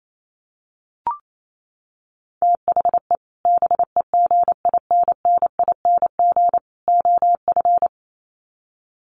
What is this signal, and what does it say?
Morse code, heard as a keyed tone, about 28 wpm, 700 Hz. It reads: THE BEGINNING OF
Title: Sets of 3 Words 28wpm